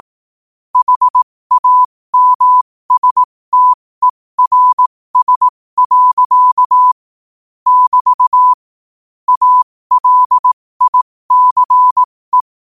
Morse code, heard as a keyed tone, about 18 wpm, spaced as for 15 wpm, 1000 Hz. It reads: HAMSTERS. <BT> ALICE
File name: qotdfiles / 15wpm